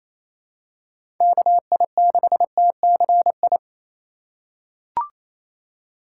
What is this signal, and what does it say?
Morse code, heard as a keyed tone, about 28 wpm, 700 Hz. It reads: KI6TCI
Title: USA Call Signs - 6 Characters - 3x Character Spacing - 28wpm